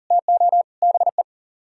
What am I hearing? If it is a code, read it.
Morse code, heard as a keyed tone, about 47 wpm, 700 Hz. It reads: TO BE
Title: Speed-Racing - Sets of 3 Words 40wpm